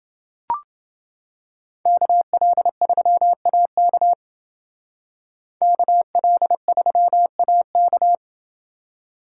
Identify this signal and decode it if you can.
Morse code, heard as a keyed tone, about 28 wpm, 700 Hz. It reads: KL3AK KL3AK